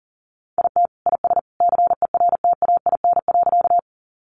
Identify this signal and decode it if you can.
Morse code, heard as a keyed tone, about 40 wpm, 700 Hz. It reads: IT IS CERTAIN.
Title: Sentences from Top 500 Words 40wpm